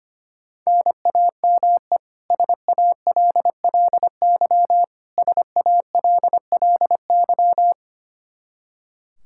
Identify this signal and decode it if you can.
Morse code, heard as a keyed tone, about 25 wpm, 700 Hz. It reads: NAME SALLY SALLY
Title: K1USN SST QSO Practice 25wpm